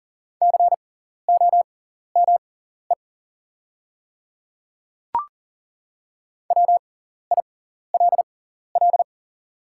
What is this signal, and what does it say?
Morse code, heard as a keyed tone, about 38 wpm, spaced as for 14 wpm, 690 Hz. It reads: COME WILL
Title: 6x Character Spacing - Top 100 Words - 40wpm